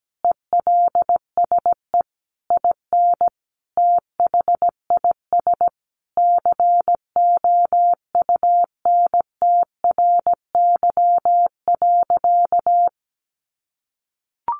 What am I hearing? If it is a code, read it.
Morse code, heard as a keyed tone, about 17 wpm, 700 Hz. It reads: ELSE IN THIS COUNTRY.
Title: Sentences from Top 1400 Words 17wpm